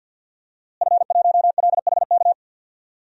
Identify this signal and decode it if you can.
Morse code, heard as a keyed tone, about 51 wpm, 700 Hz. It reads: F1LHK
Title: DXCC-100 Call Signs - 5 Characters - Repeated 2x - 50wpm